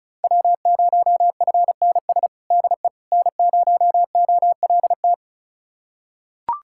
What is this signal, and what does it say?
Morse code, heard as a keyed tone, about 34 wpm, 690 Hz. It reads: W0FNS DE N0OLT
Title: USA Callsign DE Callsign - 5 Characters - 35wpm